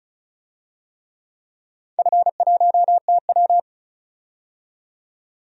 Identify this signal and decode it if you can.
Morse code, heard as a keyed tone, about 36 wpm, 700 Hz. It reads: F1TW
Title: DXCC-100 Call Signs - 4 Characters - 35wpm